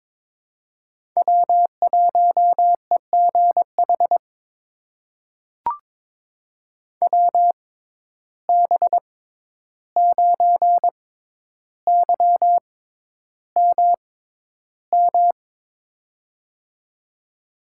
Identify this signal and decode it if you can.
Morse code, heard as a keyed tone, about 22 wpm, spaced as for 11 wpm, 700 Hz. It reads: W1EGH WB9YMM
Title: USA Call Signs - 6x Character Spacing - 22wpm